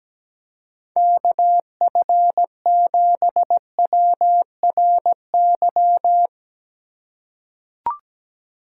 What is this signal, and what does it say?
Morse code, heard as a keyed tone, about 17 wpm, 700 Hz. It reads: KF7WRY
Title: USA Call Signs - 3x Character Spacing - 17wpm